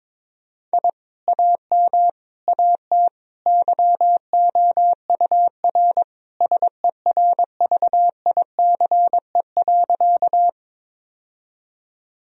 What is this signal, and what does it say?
Morse code, heard as a keyed tone, about 22 wpm, 700 Hz. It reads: I AM AT YOUR SERVICE.
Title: Sentences from Top 1000 Words - Review Part 4 - 22wpm